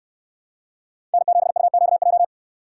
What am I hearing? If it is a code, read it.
Morse code, heard as a keyed tone, about 69 wpm, 700 Hz. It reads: N7UQJ